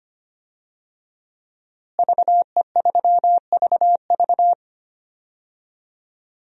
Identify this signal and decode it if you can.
Morse code, heard as a keyed tone, about 25 wpm, 700 Hz. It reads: VE3VV
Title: DXCC-100 Call Signs - 5 Characters - Repeated 3x - 25wpm